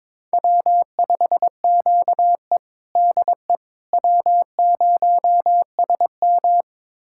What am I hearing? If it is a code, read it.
Morse code, heard as a keyed tone, about 21 wpm, 700 Hz. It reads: W5QE DE W0SM
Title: USA Callsign DE Callsign - 4 Characters - 22wpm